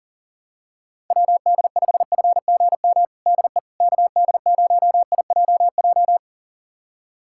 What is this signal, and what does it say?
Morse code, heard as a keyed tone, about 40 wpm, 700 Hz. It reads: WD5FGM DE KD0IJJ
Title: USA Callsign DE Callsign - 6 Characters - 40wpm